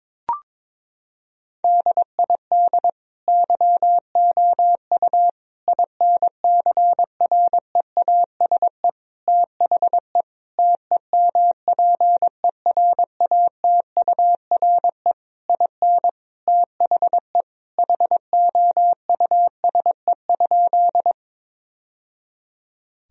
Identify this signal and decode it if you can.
Morse code, heard as a keyed tone, about 22 wpm, 700 Hz. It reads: DID YOU INCREASE THE TEMPERATURE IN THE HOUSE?
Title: Sentences from Top 1000 Words - Review Part 4 - 22wpm